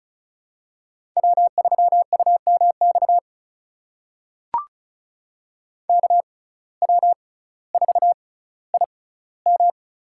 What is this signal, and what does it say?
Morse code, heard as a keyed tone, about 36 wpm, spaced as for 18 wpm, 700 Hz. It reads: W3UMX KW4IM
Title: USA Call Signs - 5 Characters - 6x Character Spacing - 35wpm